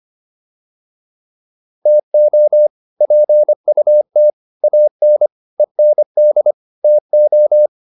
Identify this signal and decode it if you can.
Morse code, heard as a keyed tone, about 25 wpm, 600 Hz. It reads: TO PUT AN END TO